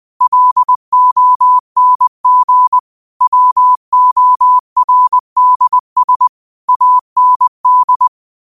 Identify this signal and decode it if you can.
Morse code, heard as a keyed tone, about 20 wpm, 1000 Hz. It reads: LONG WORDS AND